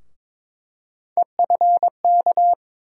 Morse code, heard as a keyed tone, about 22 wpm, 700 Hz. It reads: EFK
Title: USA Call Sign Postfixes - 3 Letters 22wpm